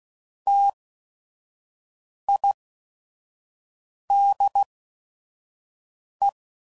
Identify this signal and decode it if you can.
Morse code, heard as a keyed tone, about 16 wpm, spaced as for 5 wpm, 800 Hz. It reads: TIDE
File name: Prefixes / By First Letter / Ts